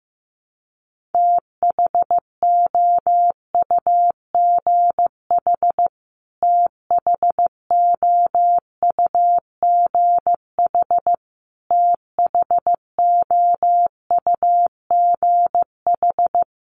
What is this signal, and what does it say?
Morse code, heard as a keyed tone, about 15 wpm, 700 Hz. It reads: THOUGH THOUGH THOUGH
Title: Top 500 Words Repeated 15wpm